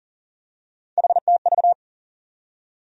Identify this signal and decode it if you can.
Morse code, heard as a keyed tone, about 40 wpm, 700 Hz. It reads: HTV
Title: USA Call Sign Postfixes - 3 Letters 40wpm